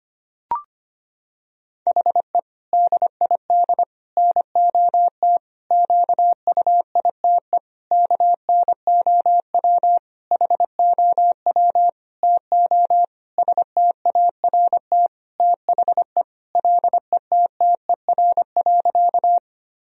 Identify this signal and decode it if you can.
Morse code, heard as a keyed tone, about 25 wpm, 700 Hz. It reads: HE DID NOT QUITE KNOW HOW TO START THE LETTER.
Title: Sentences from Top 1000 Words - Review Part 1 - 25wpm